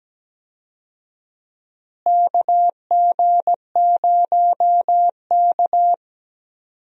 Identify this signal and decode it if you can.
Morse code, heard as a keyed tone, about 17 wpm, 700 Hz. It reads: KG0K
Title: USA Call Signs - 4 Characters - Repeated 2x - 17wpm